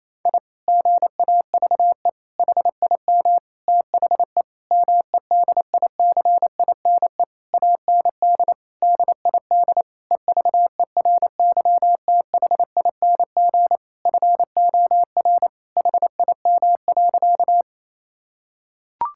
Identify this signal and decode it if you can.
Morse code, heard as a keyed tone, about 28 wpm, 700 Hz. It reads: I GAVE HIM THE MEDICINE AND DID EVERYTHING FOR HIM.
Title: Sentences from Top 1000 Words - Review Part 6 - 28wpm